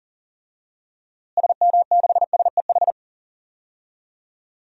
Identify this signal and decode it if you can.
Morse code, heard as a keyed tone, about 40 wpm, 700 Hz. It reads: SM6SEH
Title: DXCC-100 Call Signs - 6 Characters - Repeated 2x - 40wpm